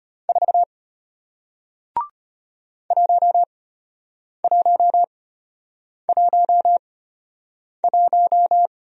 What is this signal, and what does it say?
Morse code, heard as a keyed tone, about 33 wpm, spaced as for 8 wpm, 700 Hz. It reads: 4 1111